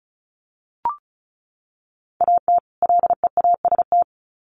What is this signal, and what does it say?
Morse code, heard as a keyed tone, about 34 wpm, 700 Hz. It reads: AT LEAST
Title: Sets of 2 Words 35wpm